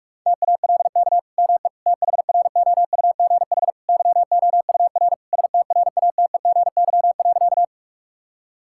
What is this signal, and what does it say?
Morse code, heard as a keyed tone, about 46 wpm, 700 Hz. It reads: TALK ME THROUGH YOUR STRATEGY.
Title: Sentences from Top 1000 Words - Review Part 5 - 45wpm